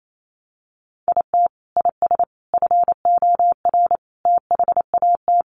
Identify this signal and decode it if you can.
Morse code, heard as a keyed tone, about 28 wpm, 700 Hz. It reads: IT IS FOR THAT
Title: Sentences from Top 700 Words 28wpm